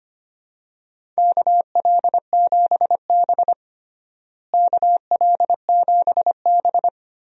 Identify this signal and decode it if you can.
Morse code, heard as a keyed tone, about 25 wpm, 700 Hz. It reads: KL7B KL7B
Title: DXCC-100 Call Signs - 4 Characters - Repeated 2x - 25wpm